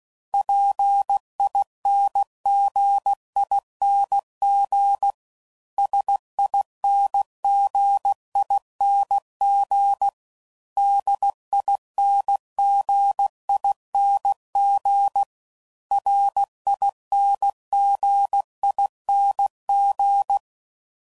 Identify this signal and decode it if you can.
Morse code, heard as a keyed tone, about 16 wpm, 800 Hz. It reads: PINGING SINGING DINGING RINGING